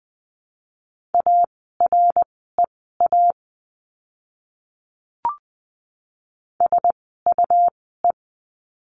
Morse code, heard as a keyed tone, about 20 wpm, spaced as for 15 wpm, 700 Hz. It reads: AREA SUE
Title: 101 Lesson 17f - Letter u - Words 20wpm